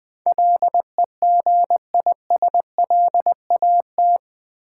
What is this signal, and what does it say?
Morse code, heard as a keyed tone, about 20 wpm, 700 Hz. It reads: LEGISLAT
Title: Build A Word - 4 Letters At A Time - 11-Letter Words - 20wpm